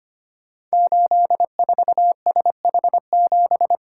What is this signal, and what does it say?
Morse code, heard as a keyed tone, about 25 wpm, 700 Hz. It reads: 84SH7
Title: Code Groups - 5 Letter:Numbers - 25wpm